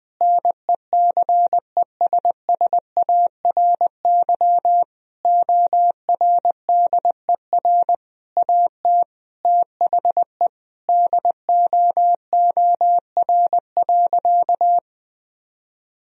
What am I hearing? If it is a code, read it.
Morse code, heard as a keyed tone, about 20 wpm, 700 Hz. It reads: NECESSARY ORDER AT THE DOOR.
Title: Sentences from Top 1000 Words - Review Part 2 - 20wpm